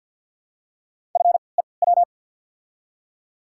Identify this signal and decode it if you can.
Morse code, heard as a keyed tone, about 47 wpm, spaced as for 28 wpm, 700 Hz. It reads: FEW
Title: 3x Character Spacing - Top 500 Words - 50wpm